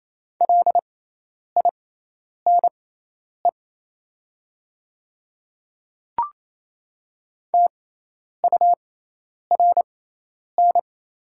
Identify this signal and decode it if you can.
Morse code, heard as a keyed tone, about 27 wpm, spaced as for 10 wpm, 700 Hz. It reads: LINE TURN
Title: 6x Character Spacing - Top 300 Words - 28wpm